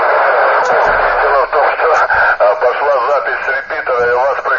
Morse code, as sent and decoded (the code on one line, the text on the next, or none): none